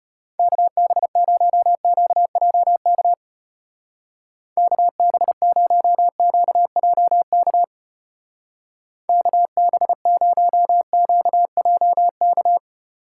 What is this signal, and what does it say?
Morse code, heard as a keyed tone, about 32 wpm, 700 Hz. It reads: KB0QJK KB0QJK KB0QJK